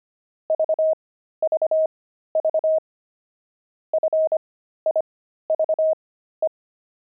Morse code, heard as a keyed tone, about 25 wpm, spaced as for 13 wpm, 630 Hz. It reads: VVV FIVE
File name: pangram-fs-13-cs-25-wine